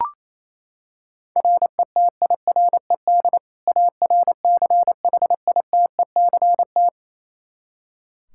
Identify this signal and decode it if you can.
Morse code, heard as a keyed tone, about 28 wpm, 700 Hz. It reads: RETIRED ARCHITECT
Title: Practice QSO - Exchange 3 - More Info - 28wpm